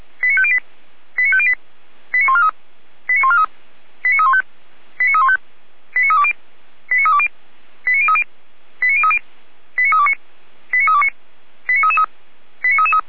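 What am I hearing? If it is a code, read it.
Five-tone selective call (ZVEI), heard as quick runs of five tones, tones of about 70 ms. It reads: 89508 89508 89143 89143 89315 89315 89320 89320 89030 89030 89329 89329 89333
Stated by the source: ZVEI five-tone selective call 89167 received by SDR, from the Feuerwehr Orlamünde audio archive.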